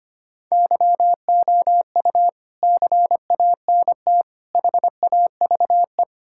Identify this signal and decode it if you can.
Morse code, heard as a keyed tone, about 25 wpm, 700 Hz. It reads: YOU CANT HAVE
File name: Sets of 9 Words - 25wpm